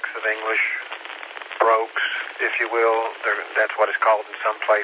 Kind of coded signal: Emergency Alert System (SAME) tones